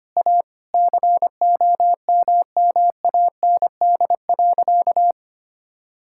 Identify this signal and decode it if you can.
Morse code, heard as a keyed tone, about 25 wpm, 700 Hz. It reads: A COMMAND.
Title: Sentences from Top 1600 Words 25wpm